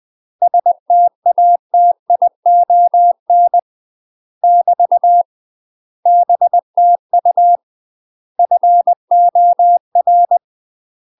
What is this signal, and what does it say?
Morse code, heard as a keyed tone, about 20 wpm, 700 Hz. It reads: STATION <BT> BTU FOR
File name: UK MASTER QSOs TRAINING / 20WPM / UK MASTER QSOs 0032